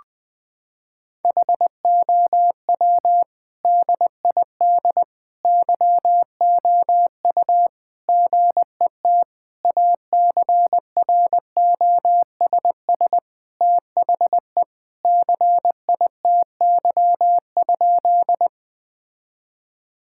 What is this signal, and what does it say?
Morse code, heard as a keyed tone, about 20 wpm, 700 Hz. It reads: HOW DID YOU GET ACROSS THE CITY?
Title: Sentences from Top 1000 Words - Review Part 5 - 20wpm